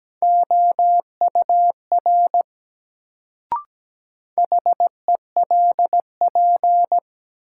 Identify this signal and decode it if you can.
Morse code, heard as a keyed tone, about 17 wpm, 700 Hz. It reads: OUR HELP
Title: Sentences from Top 200 Words - New Word Drill - 17wpm